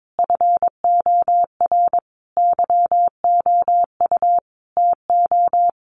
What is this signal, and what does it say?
Morse code, heard as a keyed tone, about 22 wpm, 700 Hz. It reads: FOR YOU TO
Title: Sentences from Top 1300 Words 22wpm